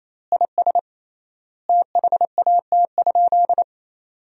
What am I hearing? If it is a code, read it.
Morse code, heard as a keyed tone, about 28 wpm, 700 Hz. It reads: IS THAT?
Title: 3x Word-Spacing - Sentences from Top 100 Words - 28wpm